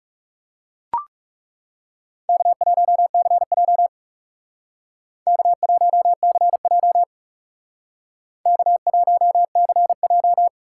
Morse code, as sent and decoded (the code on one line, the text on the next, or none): -.- .---- -.-. .--- / -.- .---- -.-. .--- / -.- .---- -.-. .---
K1CJ K1CJ K1CJ